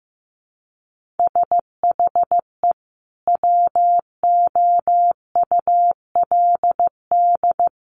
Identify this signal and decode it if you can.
Morse code, heard as a keyed tone, about 15 wpm, 700 Hz. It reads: SHE WOULD
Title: Sentences from Top 1300 Words 15wpm